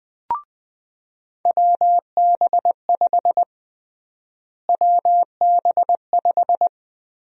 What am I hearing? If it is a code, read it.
Morse code, heard as a keyed tone, about 20 wpm, 700 Hz. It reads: WB5 WB5
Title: USA Call Sign Prefixes - Repeated 2x - 20wpm